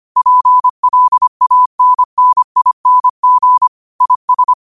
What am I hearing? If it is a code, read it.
Morse code, heard as a keyed tone, about 25 wpm, 1000 Hz. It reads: PLANNING IS
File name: qotdfiles / 25wpm